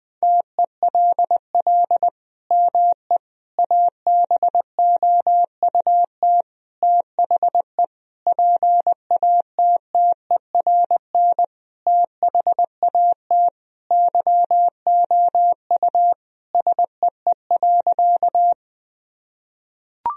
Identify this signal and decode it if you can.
Morse code, heard as a keyed tone, about 21 wpm, 700 Hz. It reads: TELL ME ABOUT THE PATTERN THAT YOU SEE.
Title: Sentences from Top 1000 Words - Review Part 2 - 20wpm